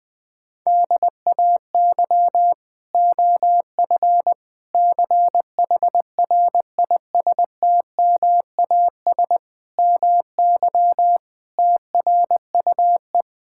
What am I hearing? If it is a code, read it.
Morse code, heard as a keyed tone, about 20 wpm, 700 Hz. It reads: DAY OF CHRISTMAS MY TRUE
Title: Sets of 8 Words - 20wpm